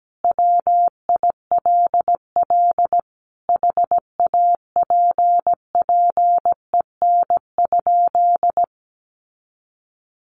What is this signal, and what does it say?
Morse code, heard as a keyed tone, about 17 wpm, 700 Hz. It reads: WILL HAPPEN?
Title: Sentences from Top 1200 Words 17wpm